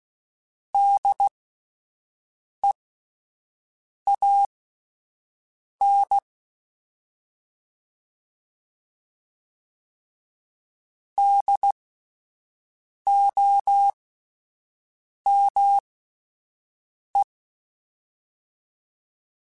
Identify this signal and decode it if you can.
Morse code, heard as a keyed tone, about 16 wpm, spaced as for 5 wpm, 800 Hz. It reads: DEAN DOME